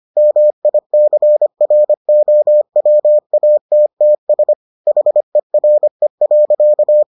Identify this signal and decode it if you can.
Morse code, heard as a keyed tone, about 25 wpm, 600 Hz. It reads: MICROWATTS HERE.